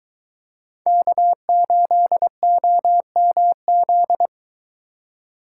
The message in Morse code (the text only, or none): K8OMZ